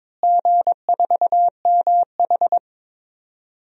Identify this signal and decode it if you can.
Morse code, heard as a keyed tone, about 22 wpm, 700 Hz. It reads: G4MH
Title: DXCC-100 Call Signs - 4 Characters - Repeated 3x - 22wpm